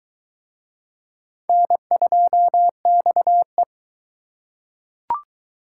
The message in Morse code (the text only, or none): N2XE